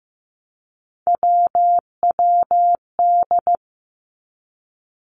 Morse code, heard as a keyed tone, about 15 wpm, 700 Hz. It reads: WWD
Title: Three Challenging Letter-Number Combinations 15wpm